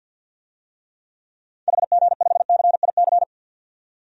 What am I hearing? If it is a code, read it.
Morse code, heard as a keyed tone, about 52 wpm, 700 Hz. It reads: HG5CIC
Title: DXCC-100 Call Signs - 6 Characters - 50wpm